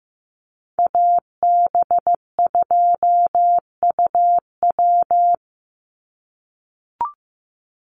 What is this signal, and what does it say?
Morse code, heard as a keyed tone, about 15 wpm, 700 Hz. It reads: AB2UW
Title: USA Call Signs 15wpm